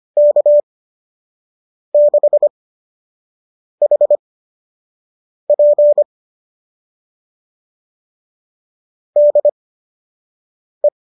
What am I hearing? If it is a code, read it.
Morse code, heard as a keyed tone, about 25 wpm, spaced as for 6 wpm, 590 Hz. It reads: K6HP DE
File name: qso003 06